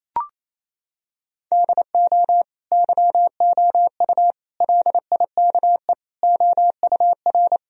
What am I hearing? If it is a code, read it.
Morse code, heard as a keyed tone, about 28 wpm, 700 Hz. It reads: DO YOU LIKE OUR
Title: Sentences from Top 1000 Words - Review Part 1 - 28wpm